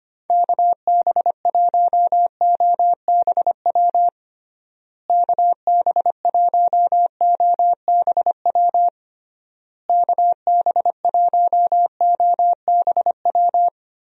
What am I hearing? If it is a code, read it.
Morse code, heard as a keyed tone, about 25 wpm, 700 Hz. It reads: KB1OBW KB1OBW KB1OBW
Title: USA Call Signs - 6 Characters - Repeated 3x - 25wpm